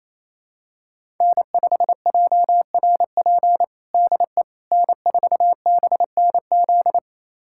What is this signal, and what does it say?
Morse code, heard as a keyed tone, about 27 wpm, 690 Hz. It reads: N5JRP DE N4BNZ
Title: USA Callsign DE Callsign - 5 Characters - 28wpm